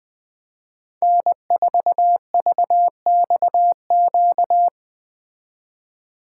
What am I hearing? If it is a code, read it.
Morse code, heard as a keyed tone, about 20 wpm, 700 Hz. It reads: N4VXQ